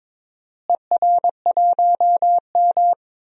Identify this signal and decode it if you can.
Morse code, heard as a keyed tone, about 22 wpm, 700 Hz. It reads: ER1M
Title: DXCC-100 Call Signs - 4 Characters - Repeated 3x - 22wpm